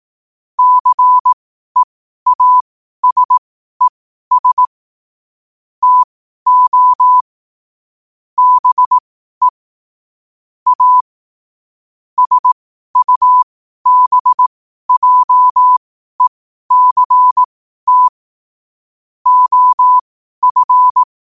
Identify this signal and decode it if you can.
Morse code, heard as a keyed tone, about 18 wpm, spaced as for 12 wpm, 1000 Hz. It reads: CEASES TO BE A SUBJECT OF